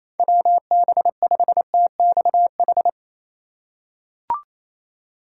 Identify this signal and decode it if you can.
Morse code, heard as a keyed tone, about 28 wpm, 700 Hz. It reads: WB5TXH
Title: USA Call Signs - 6 Characters - 3x Character Spacing - 28wpm